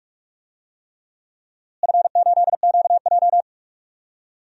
Morse code, heard as a keyed tone, about 43 wpm, 690 Hz. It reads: F8QJ
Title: DXCC-100 Call Signs - 4 Characters - Repeated 3x - 45wpm